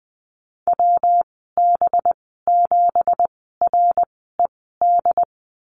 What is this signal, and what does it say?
Morse code, heard as a keyed tone, about 20 wpm, spaced as for 14 wpm, 700 Hz. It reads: WB7RED
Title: 186 Lesson 30g - Letter b - Call Signs 20wpm